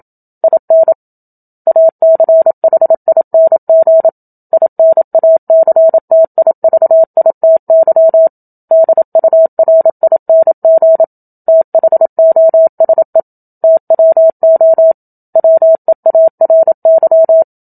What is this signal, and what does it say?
Morse code, heard as a keyed tone, about 27 wpm, 650 Hz. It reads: IN ACHING INACTIVITY DURING THOSE TWO WEARY
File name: Book Two / 01-Chapter 1 - 27 WPM